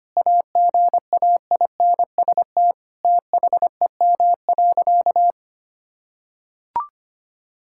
Morse code, heard as a keyed tone, about 25 wpm, 700 Hz. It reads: AGAINST THEM.
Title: Sentences from Top 1000 Words - Review Part 6 - 25wpm